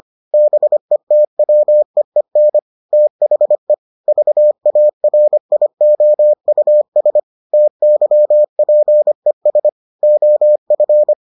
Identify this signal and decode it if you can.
Morse code, heard as a keyed tone, about 25 wpm, 600 Hz. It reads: BETWEEN THE VARIOUS TYPES OF